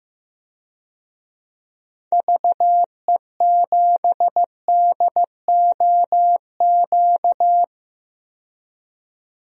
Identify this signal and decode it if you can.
Morse code, heard as a keyed tone, about 15 wpm, 700 Hz. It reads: VE7DOQ
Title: DXCC-100 Call Signs - 6 Characters - 15wpm